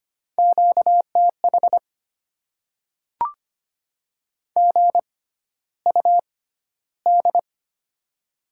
Morse code, heard as a keyed tone, about 25 wpm, spaced as for 13 wpm, 700 Hz. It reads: QTH GUD
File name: Top 20 QSO Elements - 6x Character Spacing - 25wpm